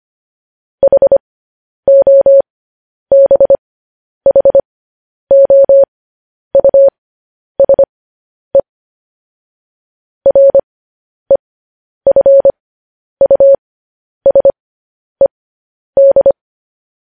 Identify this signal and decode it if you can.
Morse code, heard as a keyed tone, about 25 wpm, spaced as for 10 wpm, 550 Hz. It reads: HOBHOUSE REFUSED
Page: news headlines in morse code at 10 wpm